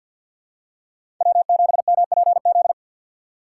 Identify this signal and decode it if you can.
Morse code, heard as a keyed tone, about 53 wpm, 700 Hz. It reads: W7MPZ
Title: USA Call Signs - 5 Characters - 4x Character Spacing - 50wpm